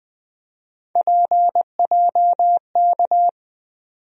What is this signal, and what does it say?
Morse code, heard as a keyed tone, about 20 wpm, 700 Hz. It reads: PJK
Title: USA Call Sign Postfixes - 3 Letters 20wpm